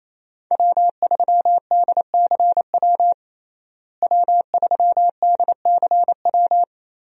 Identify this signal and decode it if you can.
Morse code, heard as a keyed tone, about 28 wpm, 700 Hz. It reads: W3DCW W3DCW
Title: USA Call Signs - 5 Characters - Repeated 3x - 28wpm